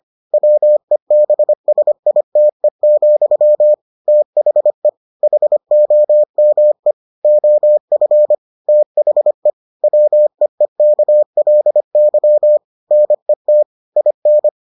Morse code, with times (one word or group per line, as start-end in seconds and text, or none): 0.3-3.7 WEBSITE,
4.1-4.9 THE
5.2-6.9 HOME
7.2-8.3 OF
8.7-9.5 THE
9.8-12.6 WEEKLY
12.9-13.6 NET
14.0-14.5 IN